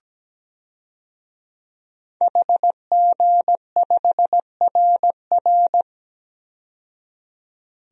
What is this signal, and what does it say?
Morse code, heard as a keyed tone, about 17 wpm, 700 Hz. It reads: HG5RR
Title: DXCC-100 Call Signs - 5 Characters - 17wpm